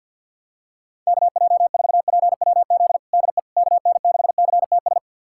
Morse code, heard as a keyed tone, about 50 wpm, 700 Hz. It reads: KJ4PWZ DE KN6CTS
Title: USA Callsign DE Callsign 50wpm